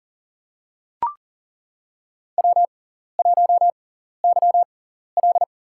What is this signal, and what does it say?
Morse code, heard as a keyed tone, about 40 wpm, spaced as for 14 wpm, 700 Hz. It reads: W1YL
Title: USA Call Signs - 4 Characters - 6x Character Spacing - 40wpm